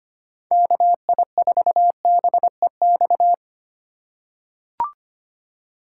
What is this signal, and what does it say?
Morse code, heard as a keyed tone, about 25 wpm, 700 Hz. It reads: KI4BEX
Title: USA Call Signs - 6 Characters - 4x Character Spacing - 25wpm